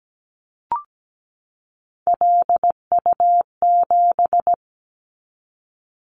Morse code, heard as a keyed tone, about 17 wpm, 700 Hz. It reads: LU7
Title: Callsign Prefixes for Top 50 DXCC - 17wpm